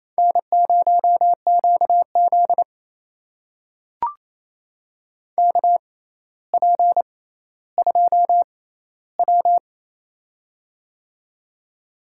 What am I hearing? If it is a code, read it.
Morse code, heard as a keyed tone, about 27 wpm, spaced as for 14 wpm, 700 Hz. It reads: N0QZ KP2W
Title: USA Call Signs - 4 Characters - 6x Character Spacing - 28wpm